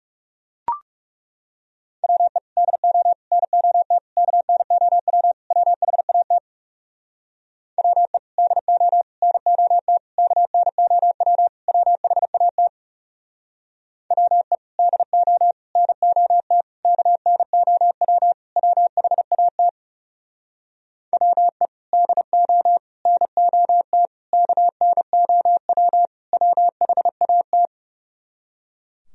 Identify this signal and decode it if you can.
Morse code, heard as a keyed tone, about 36 wpm, 700 Hz. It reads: WE DO NOT KNOW WHAT WE DO NOT KNOW WHAT WE DO NOT KNOW WHAT WE DO NOT KNOW WHAT
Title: Speed-Racing - Sets of 5 Words 30wpm